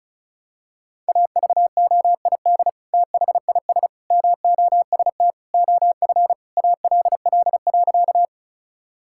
Morse code, heard as a keyed tone, about 36 wpm, 700 Hz. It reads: AVOID THIS MOST OF ALL.
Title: Sentences from Top 1000 Words - Review Part 6 - 35wpm